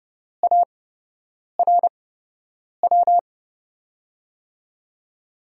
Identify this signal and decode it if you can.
Morse code, heard as a keyed tone, about 30 wpm, spaced as for 8 wpm, 700 Hz. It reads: ARW
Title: USA Call Signs - 8x Character Spacing - 30wpm